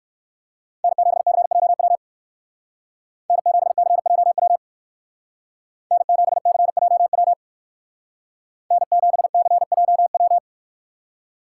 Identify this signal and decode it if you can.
Morse code, heard as a keyed tone, about 58 wpm, 700 Hz. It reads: N7CJW N7CJW N7CJW N7CJW